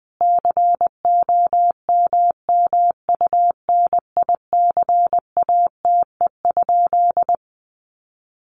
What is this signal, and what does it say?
Morse code, heard as a keyed tone, about 20 wpm, 700 Hz. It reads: COMMUNICATE?
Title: Sentences from Top 1300 Words 20wpm